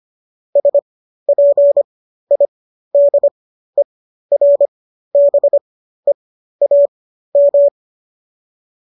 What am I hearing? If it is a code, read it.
Morse code, heard as a keyed tone, about 25 wpm, spaced as for 13 wpm, 570 Hz. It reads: SPIDERBEAM
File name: qso214 13